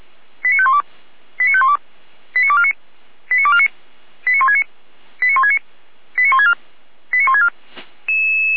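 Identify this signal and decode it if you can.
Five-tone selective call (ZVEI), heard as quick runs of five tones, tones of about 80 ms. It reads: 89512 89512 89249 89249 89169 89169 89164 89164
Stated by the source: ZVEI five-tone selective call received on an SDR.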